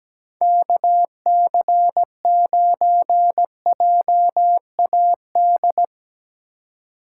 Morse code, heard as a keyed tone, about 17 wpm, 700 Hz. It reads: KC9JAD
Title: USA Call Signs - 6 Characters - 3x Character Spacing - 17wpm